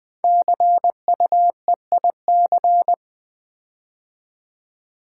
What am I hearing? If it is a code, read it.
Morse code, heard as a keyed tone, about 20 wpm, 700 Hz. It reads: CUEIC
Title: Code Groups - 5 Letter:Numbers - 20wpm